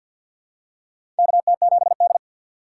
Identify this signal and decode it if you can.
Morse code, heard as a keyed tone, about 52 wpm, 700 Hz. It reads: KT7D
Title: USA Call Signs - 4 Characters - 2x Character Spacing - 50wpm